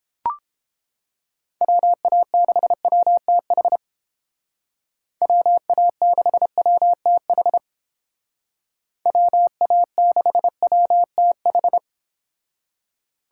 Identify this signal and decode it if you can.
Morse code, heard as a keyed tone, about 31 wpm, 700 Hz. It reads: WA6WTH WA6WTH WA6WTH